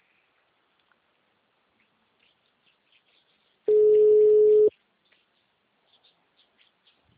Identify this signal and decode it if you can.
Ringback tone (Europe, 425 Hz).
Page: Dourados - MS